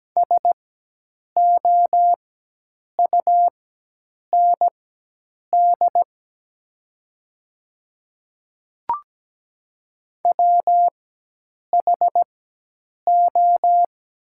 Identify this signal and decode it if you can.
Morse code, heard as a keyed tone, about 17 wpm, spaced as for 8 wpm, 700 Hz. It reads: SOUND WHO
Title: 4x Character Spacing - Top 100 Words - 17wpm